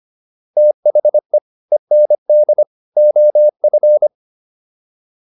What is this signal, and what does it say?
Morse code, heard as a keyed tone, about 25 wpm, 600 Hz. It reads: THE END OF